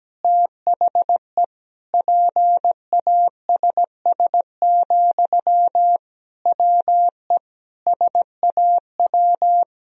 Morse code, heard as a keyed tone, about 17 wpm, 700 Hz. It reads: THE PASS, WE SAW
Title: Sentences from Top 1000 Words - Review Part 2 - 17wpm